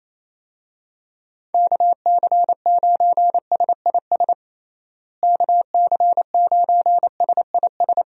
Morse code, heard as a keyed tone, about 28 wpm, 700 Hz. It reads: KC9SIS KC9SIS
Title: USA Call Signs - 6 Characters - Repeated 2x - 28wpm